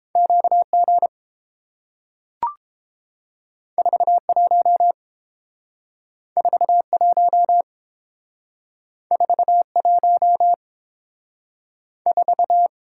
Morse code, heard as a keyed tone, about 30 wpm, 700 Hz. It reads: QG 41 41 41 4